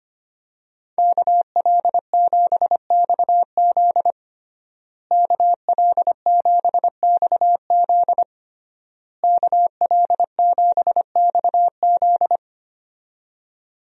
Morse code, heard as a keyed tone, about 25 wpm, 700 Hz. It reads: KL7XZ KL7XZ KL7XZ
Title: DXCC-100 Call Signs - 5 Characters - Repeated 3x - 25wpm